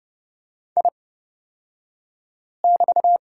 Morse code, heard as a keyed tone, about 30 wpm, spaced as for 5 wpm, 700 Hz. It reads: I<BT>
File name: Lesson 36d - BT - Two Characters 30wpm